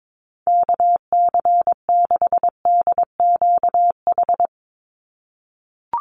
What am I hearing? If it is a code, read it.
Morse code, heard as a keyed tone, about 22 wpm, 700 Hz. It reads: KC6DQH
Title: USA Call Signs 22wpm